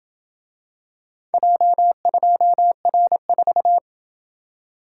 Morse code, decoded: J2R4

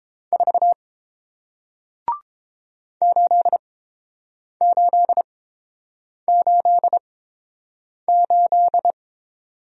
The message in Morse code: 4 8888

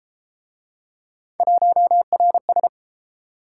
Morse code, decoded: 1RS

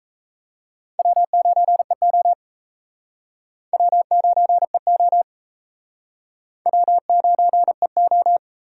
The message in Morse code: W9EO W9EO W9EO

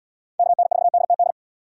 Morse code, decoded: KN4DTD

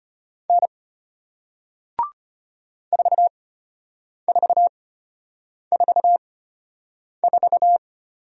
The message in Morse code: N 4444

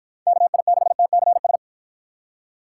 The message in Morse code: KI6TCS

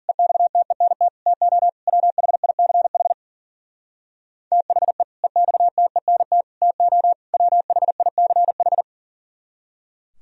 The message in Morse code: EXTENT TO WHICH THE EXTENT TO WHICH

